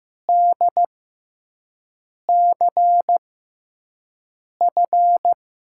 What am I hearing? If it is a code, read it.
Morse code, heard as a keyed tone, about 15 wpm, spaced as for 5 wpm, 700 Hz. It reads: DCF
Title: USA Call Signs - 6 Characters - 6x Character Spacing - 15wpm